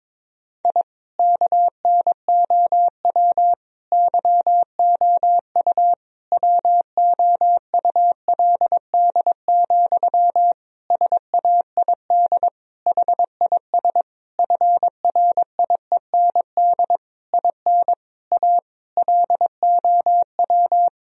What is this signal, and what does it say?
Morse code, heard as a keyed tone, about 22 wpm, 700 Hz. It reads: I KNOW YOU WOULD, SAID HIS FRIEND IN A LOW
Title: Sentences from Top 1000 Words - Review Part 1 - 22wpm